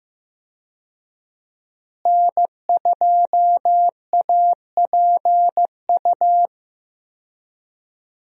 Morse code, heard as a keyed tone, about 15 wpm, 700 Hz. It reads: N2APU
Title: USA Call Signs - 5 Characters - Repeated 2x - 15wpm